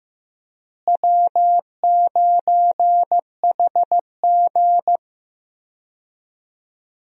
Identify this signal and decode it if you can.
Morse code, heard as a keyed tone, about 15 wpm, 700 Hz. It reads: W9HG
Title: USA Call Signs - 4 Characters - Repeated 3x - 15wpm